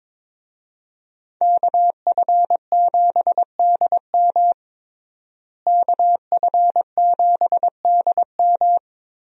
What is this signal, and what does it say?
Morse code, heard as a keyed tone, about 21 wpm, 700 Hz. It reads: KF7DM KF7DM
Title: USA Call Signs - 5 Characters - Repeated 3x - 22wpm